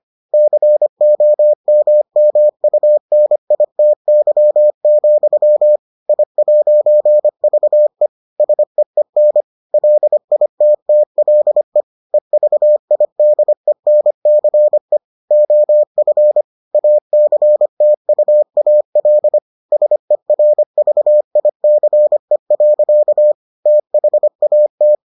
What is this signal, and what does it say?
Morse code, heard as a keyed tone, about 25 wpm, 600 Hz. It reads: COMMUNITY, I'VE SEEN LITTLE EVIDENCE OF ACTUAL SERVICE. THAT